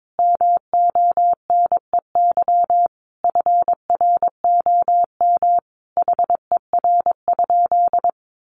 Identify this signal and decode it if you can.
Morse code, heard as a keyed tone, about 22 wpm, 700 Hz. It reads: MONEY FROM HER?
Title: Sentences from Top 1500 Words 22wpm